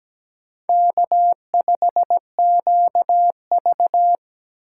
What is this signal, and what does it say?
Morse code, heard as a keyed tone, about 17 wpm, 700 Hz. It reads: K5QV
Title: USA Call Signs - 4 Characters - 4x Character Spacing - 17wpm